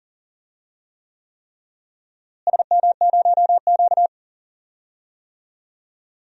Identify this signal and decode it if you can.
Morse code, heard as a keyed tone, about 40 wpm, 700 Hz. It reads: SM0Q
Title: DXCC-100 Call Signs - 4 Characters - Repeated 3x - 40wpm